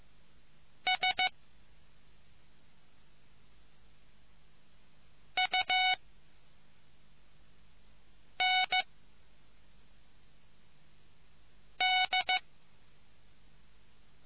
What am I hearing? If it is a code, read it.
Morse code, heard as a keyed tone, about 15 wpm, 740 Hz. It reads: S U N D